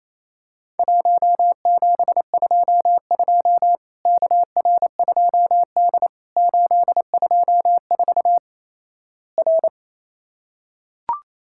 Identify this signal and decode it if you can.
Morse code, heard as a keyed tone, about 28 wpm, 700 Hz. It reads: 1722 KR2D 824 R
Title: Worked All Europe DX Contest - QTC Exchange - 28wpm